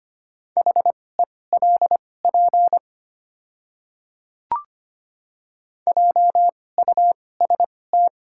Morse code, heard as a keyed tone, about 25 wpm, spaced as for 18 wpm, 700 Hz. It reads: HELP JUST
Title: 2x Character Spacing - Top 200 Words - 25wpm